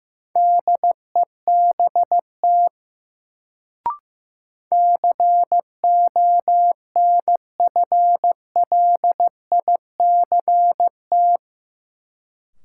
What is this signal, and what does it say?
Morse code, heard as a keyed tone, about 15 wpm, 700 Hz. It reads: DEBT CONFLICT
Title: Sentences from Top 1300 Words - New Word Drill - 15wpm